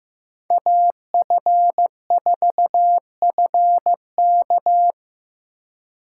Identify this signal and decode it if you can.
Morse code, heard as a keyed tone, about 18 wpm, 700 Hz. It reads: AF4FK
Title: Speed Racing - USA 2x2 Call Signs 15wpm